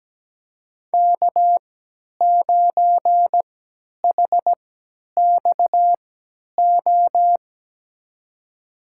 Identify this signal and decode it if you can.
Morse code, heard as a keyed tone, about 17 wpm, spaced as for 10 wpm, 700 Hz. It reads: K9HXO
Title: USA Call Signs - 5 Characters - 3x Character Spacing - 17wpm